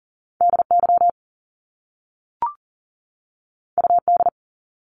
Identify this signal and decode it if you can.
Morse code, heard as a keyed tone, about 40 wpm, 700 Hz. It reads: DY UD
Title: Top Two Letter Combinations in Words 40wpm